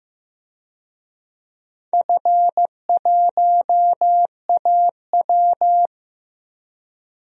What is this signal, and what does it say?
Morse code, heard as a keyed tone, about 15 wpm, 700 Hz. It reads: F1AW
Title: DXCC-100 Call Signs - 4 Characters - Repeated 2x - 15wpm